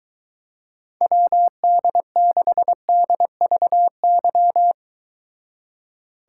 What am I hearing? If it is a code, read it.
Morse code, heard as a keyed tone, about 22 wpm, 700 Hz. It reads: WD6DVY